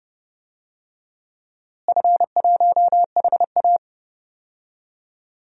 Morse code, heard as a keyed tone, about 30 wpm, 700 Hz. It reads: F1HA
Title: DXCC-100 Call Signs - 4 Characters - Repeated 2x - 30wpm